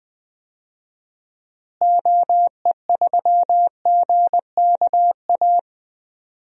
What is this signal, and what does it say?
Morse code, heard as a keyed tone, about 20 wpm, 700 Hz. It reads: OE3GKA
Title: DXCC-100 Call Signs - 6 Characters - Repeated 2x - 20wpm